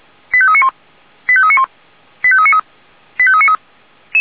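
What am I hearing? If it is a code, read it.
Five-tone selective call (ZVEI), heard as quick runs of five tones, tones of about 70 ms. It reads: 85391 85391 85393 85393
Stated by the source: ZVEI five-tone selective call 89485 received by SDR.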